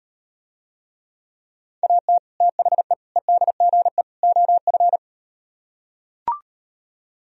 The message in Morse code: AT THE EDGE OF

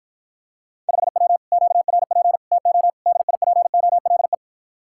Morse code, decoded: 5W QRP TO DIPOLE

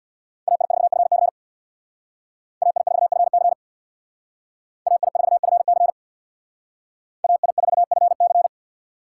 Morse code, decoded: AI4FC AI4FC AI4FC AI4FC

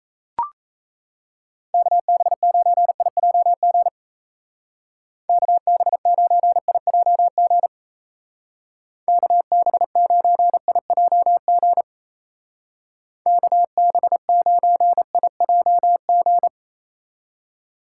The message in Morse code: KB9IJG KB9IJG KB9IJG KB9IJG

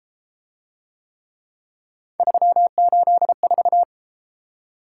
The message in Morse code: 384

